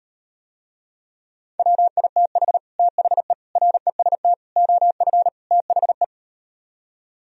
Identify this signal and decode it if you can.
Morse code, heard as a keyed tone, about 38 wpm, 700 Hz. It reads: WITH THE REST OF THE